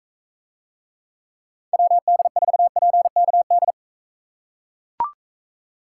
Morse code, heard as a keyed tone, about 44 wpm, 700 Hz. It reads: WD4PKD